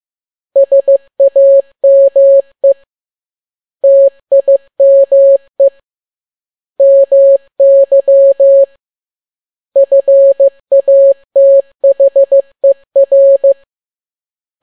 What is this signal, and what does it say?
Morse code, heard as a keyed tone, about 15 wpm, 550 Hz. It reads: SAME TIME MY FATHER
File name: n6tt p1 15C13